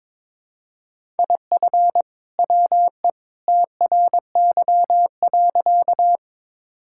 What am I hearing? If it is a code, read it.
Morse code, heard as a keyed tone, about 23 wpm, 700 Hz. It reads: IF WE TRY.
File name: practice / 22wpm / Sentences with 2-Word Phrases - 22wpm